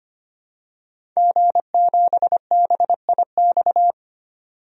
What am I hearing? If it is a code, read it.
Morse code, heard as a keyed tone, about 25 wpm, 700 Hz. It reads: G7BIX
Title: DXCC-100 Call Signs - 5 Characters - 25wpm